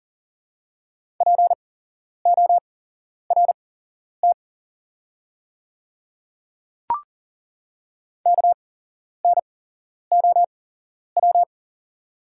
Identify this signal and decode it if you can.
Morse code, heard as a keyed tone, about 37 wpm, spaced as for 11 wpm, 700 Hz. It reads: PORT KNOW
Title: 8x Character Spacing - Top 300 Words - 40wpm